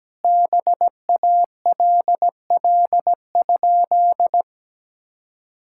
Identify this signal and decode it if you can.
Morse code, heard as a keyed tone, about 17 wpm, 700 Hz. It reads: BALL?
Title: Sentences from Top 1000 Words - Review Part 6 - 17wpm